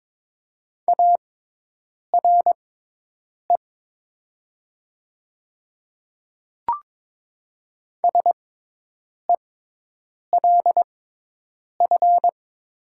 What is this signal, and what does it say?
Morse code, heard as a keyed tone, about 22 wpm, spaced as for 8 wpm, 700 Hz. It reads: ARE SELF
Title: 6x Character Spacing - Top 200 Words - 22wpm